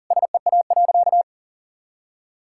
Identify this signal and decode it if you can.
Morse code, heard as a keyed tone, about 40 wpm, 700 Hz. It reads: SEA.
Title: Sentences from Top 1000 Words - Review Part 5 - 40wpm